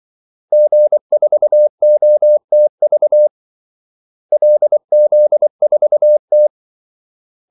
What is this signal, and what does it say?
Morse code, heard as a keyed tone, about 24 wpm, 600 Hz. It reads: G4OTV LZ4T
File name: Callsigns-24wpm